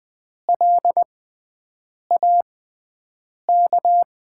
Morse code, heard as a keyed tone, about 20 wpm, spaced as for 7 wpm, 700 Hz. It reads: LAK
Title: USA Call Signs - 5 Characters - 6x Character Spacing - 20wpm